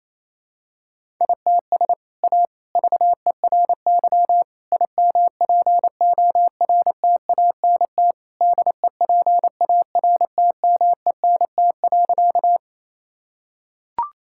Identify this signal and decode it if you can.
Morse code, heard as a keyed tone, about 28 wpm, 700 Hz. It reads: ITS A VERY IMPORTANT DEPARTMENT.
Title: Sentences from Top 1000 Words - Review Part 5 - 28wpm